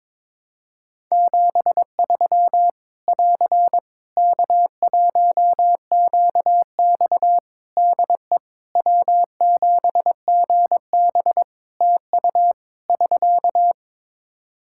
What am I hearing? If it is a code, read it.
Morse code, heard as a keyed tone, about 22 wpm, 700 Hz. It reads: 73 <AR> K1QX DE W7GB TU <SK>
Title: K1USN SST QSO Practice 22wpm